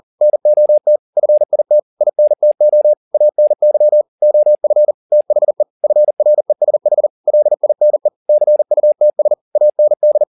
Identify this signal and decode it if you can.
Morse code, heard as a keyed tone, about 40 wpm, 600 Hz. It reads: NOT FIT INTO ANY OF THE FRESH LINE CUTS AND
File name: the old man and the sea 2 40wpm